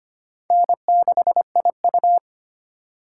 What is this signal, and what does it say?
Morse code, heard as a keyed tone, about 31 wpm, 700 Hz. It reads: N6IU
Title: Speed Racing - USA 1x2 Call Signs 25wpm